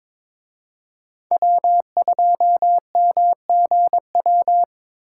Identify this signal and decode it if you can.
Morse code, heard as a keyed tone, about 26 wpm, 700 Hz. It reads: W2MGW
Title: Speed Racing - USA 1x3 Call Signs 22wpm